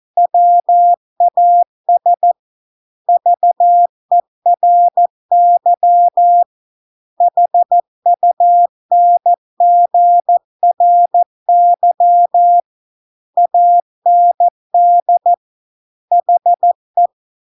Morse code, heard as a keyed tone, about 14 wpm, 700 Hz. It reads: WAS VERY HUNGRY AND HE